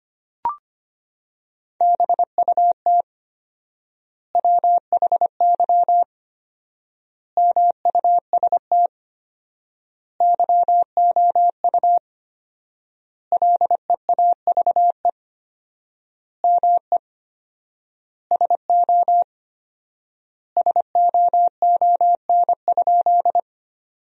Morse code, heard as a keyed tone, about 25 wpm, 700 Hz. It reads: BUT WHY MUST YOU LEAVE ME SO SOON?
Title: 4x Word-Spacing - Sentences from Top 400 Words - 25wpm